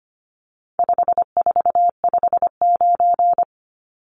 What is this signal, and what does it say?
Morse code, heard as a keyed tone, about 25 wpm, 700 Hz. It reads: 5459
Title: Numbers - 4-Digit 25wpm